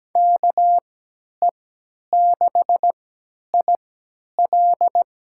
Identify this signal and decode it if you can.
Morse code, heard as a keyed tone, about 17 wpm, spaced as for 10 wpm, 700 Hz. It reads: KE6IL
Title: USA Call Signs - 3x Character Spacing - 17wpm